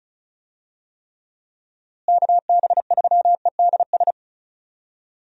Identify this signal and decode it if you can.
Morse code, heard as a keyed tone, about 36 wpm, 700 Hz. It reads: KB3EDS
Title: USA Call Signs - 6 Characters - Repeated 3x - 35wpm